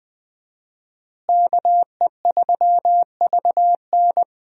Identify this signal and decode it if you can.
Morse code, heard as a keyed tone, about 20 wpm, 700 Hz. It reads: KE3VN
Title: USA Call Signs - 5 Characters - Repeated 3x - 20wpm